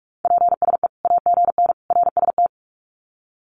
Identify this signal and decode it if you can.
Morse code, heard as a keyed tone, about 45 wpm, 700 Hz. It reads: PSE AGN RST
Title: Common QSO Phrases 45wpm